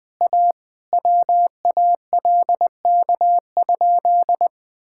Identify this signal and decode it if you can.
Morse code, heard as a keyed tone, about 20 wpm, 700 Hz. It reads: A WALK?
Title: Sentences with 3-Word Phrases - 20wpm